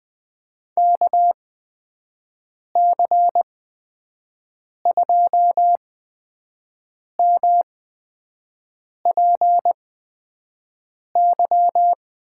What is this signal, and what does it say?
Morse code, heard as a keyed tone, about 20 wpm, spaced as for 5 wpm, 700 Hz. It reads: KC2MPY
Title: USA Call Signs - 6 Characters - 8x Character Spacing - 20wpm